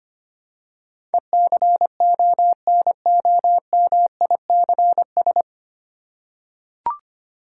Morse code, heard as a keyed tone, about 25 wpm, 700 Hz. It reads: ECONOMICS
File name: practice / 25wpm / 9 Letter Words - First 3 Letters Given - 25wpm